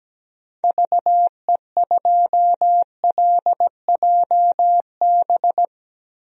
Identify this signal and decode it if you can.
Morse code, heard as a keyed tone, about 17 wpm, 700 Hz. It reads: VE2LJB
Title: DXCC-100 Call Signs - 6 Characters - Repeated 2x - 17wpm